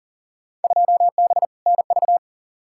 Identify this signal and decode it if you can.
Morse code, heard as a keyed tone, about 39 wpm, 700 Hz. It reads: 2B NV
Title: Field Day Exchanges Encore - 40wpm